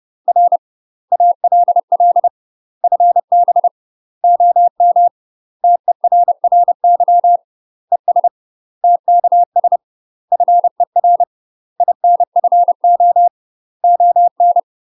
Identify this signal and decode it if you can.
Morse code, heard as a keyed tone, about 30 wpm, 700 Hz. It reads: R ALL FB OM TERRY ES TKS FER INFO ON
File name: UK MASTER QSOs TRAINING / 30WPM / UK MASTER QSOs 0016